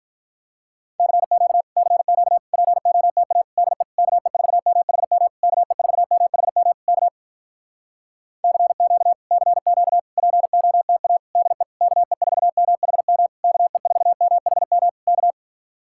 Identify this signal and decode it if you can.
Morse code, heard as a keyed tone, about 46 wpm, 700 Hz. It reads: CQ CQ POTA DE KE4MHM KE4MHM K CQ CQ POTA DE KE4MHM KE4MHM K